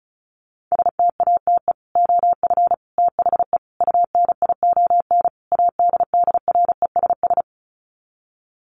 Morse code, heard as a keyed tone, about 34 wpm, 700 Hz. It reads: STATE OF THE UNION ADDRESS
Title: Sets of 5 Words Encore 35wpm